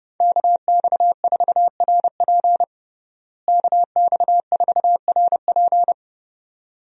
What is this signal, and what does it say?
Morse code, heard as a keyed tone, about 32 wpm, 700 Hz. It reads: KX4RP KX4RP
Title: USA Call Signs - 5 Characters - Repeated 2x - 30wpm